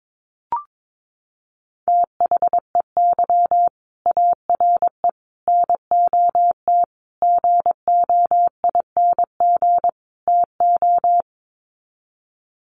Morse code, heard as a keyed tone, about 22 wpm, 700 Hz. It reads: THEY ARE NOT GOING TO
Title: Sets of 5 Words Encore2 22wpm